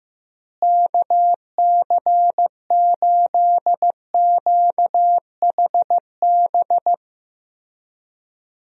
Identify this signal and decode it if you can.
Morse code, heard as a keyed tone, about 15 wpm, 700 Hz. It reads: KC8QHB
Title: USA Call Signs - 6 Characters - Repeated 3x - 15wpm